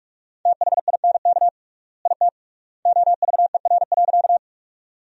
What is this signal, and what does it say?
Morse code, heard as a keyed tone, about 44 wpm, 700 Hz. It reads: THINK IT OVER.
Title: 3x Word-Spacing - Sentences from Top 200 Words - 45wpm